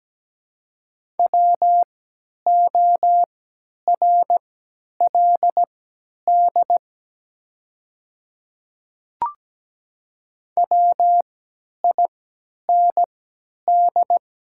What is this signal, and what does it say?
Morse code, heard as a keyed tone, about 17 wpm, spaced as for 10 wpm, 700 Hz. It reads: WORLD WIND
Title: 3x Character Spacing - Top 500 Words - 17wpm